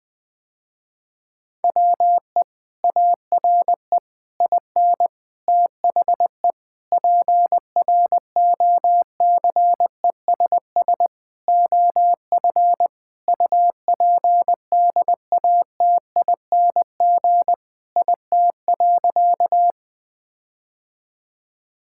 Morse code, heard as a keyed tone, about 19 wpm, 700 Hz. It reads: WE ARE IN THE PROCESS OF UPDATING IT.
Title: Sentences with 6-Word Phrases - Spoken Beginning - 20wpm